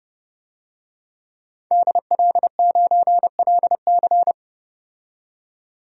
Morse code, heard as a keyed tone, about 30 wpm, 700 Hz. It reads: DL9LC
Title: DXCC-100 Call Signs - 5 Characters - 30wpm